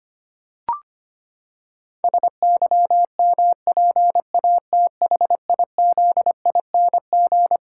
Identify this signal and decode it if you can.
Morse code, heard as a keyed tone, about 25 wpm, 700 Hz. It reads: SYMPATHIZING
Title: Pangram Challenge v2 - 25wpm